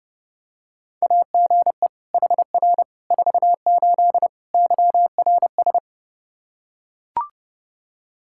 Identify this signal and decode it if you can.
Morse code, heard as a keyed tone, about 31 wpm, 700 Hz. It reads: AGE HR 48 YRS
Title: Practice QSO - Exchange 3 - More Info - 30wpm